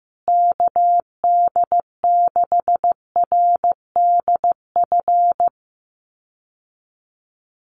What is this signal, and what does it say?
Morse code, heard as a keyed tone, about 15 wpm, 700 Hz. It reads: KD6RDF
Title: USA Call Signs Repeated 3x 15wpm